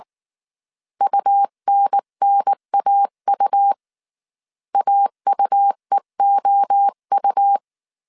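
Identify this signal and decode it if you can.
Morse code, heard as a keyed tone, about 19 wpm, 800 Hz. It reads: UNNAU AUEOU
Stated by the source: Morse code dictation at 18 wpm from the IW3SGT QRS Morse code training course, five-character groups of 300 characters.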